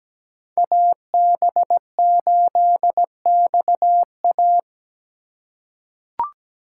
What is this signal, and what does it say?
Morse code, heard as a keyed tone, about 17 wpm, 700 Hz. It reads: AB8XA
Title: USA Call Signs - 8x Character Spacing - 17wpm